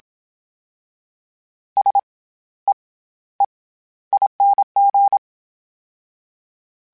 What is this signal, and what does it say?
Morse code, heard as a keyed tone, about 27 wpm, 800 Hz. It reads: S E E ING
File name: Special INGs-mp3s / Special INGs-mp3s / SlowToFaster ing @25